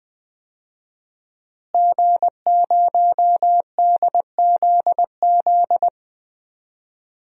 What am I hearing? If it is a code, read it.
Morse code, heard as a keyed tone, about 20 wpm, 700 Hz. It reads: G0DZZ
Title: DXCC-100 Call Signs - 5 Characters - Repeated 3x - 20wpm